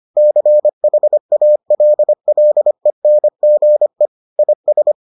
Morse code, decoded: CHALLENGE IS

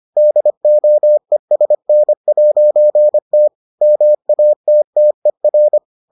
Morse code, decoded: DOESN'T MATTER

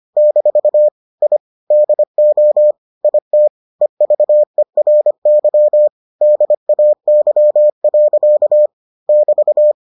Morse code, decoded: - I DO IT EVERY DAY. <BT>

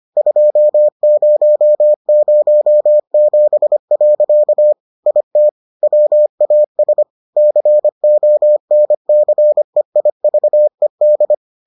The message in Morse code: 2007. IT WAS CONCEIVED

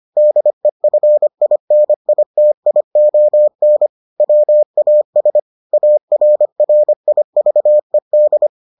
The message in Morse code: DEFINITION WAS ARRIVED

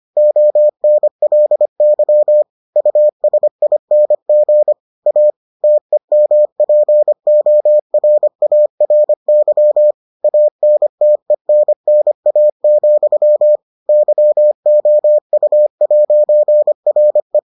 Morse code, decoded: ONLY USING A TEMPORARY ANTENNA, YOU'RE

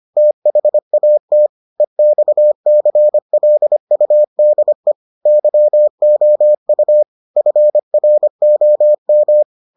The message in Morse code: THAT EXCLUDE YOU FROM

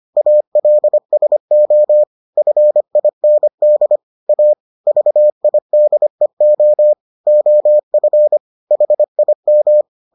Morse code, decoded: ALSO FIND A VIDEO OF HIM

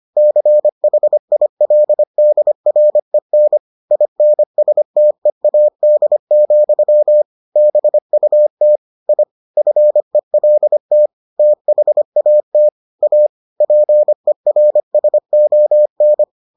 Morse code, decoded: CHILDREN INSTEAD, BUT I FELT THAT A PERSON

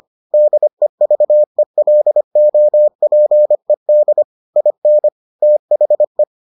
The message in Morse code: DEVELOPED IN THE

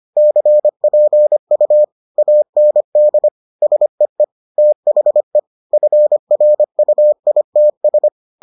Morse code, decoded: CPU AND SEE THE FRUITS

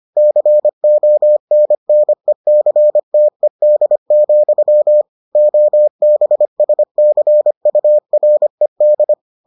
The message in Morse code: CONNECTED, OBSCURED

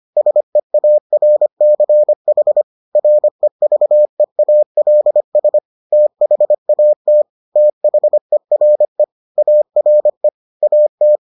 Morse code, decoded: SEARCH REVEALS THAT THERE ARE AT

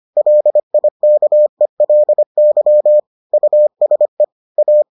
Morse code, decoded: LIKELY USE A

